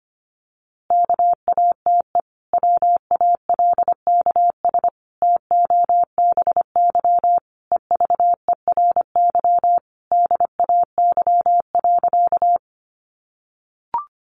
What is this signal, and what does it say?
Morse code, heard as a keyed tone, about 25 wpm, 700 Hz. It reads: KATE WALKS TOBY EVERY DAY.
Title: Easy Phrases 25wpm